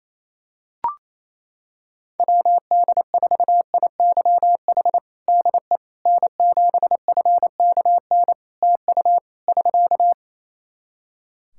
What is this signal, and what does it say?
Morse code, heard as a keyed tone, about 28 wpm, 700 Hz. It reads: WD4IYH DE N7FKN TU <SK>
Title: Practice QSO - Exchange 4 - Ending - 28wpm